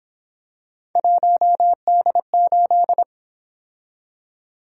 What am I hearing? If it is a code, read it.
Morse code, heard as a keyed tone, about 29 wpm, 700 Hz. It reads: 1D8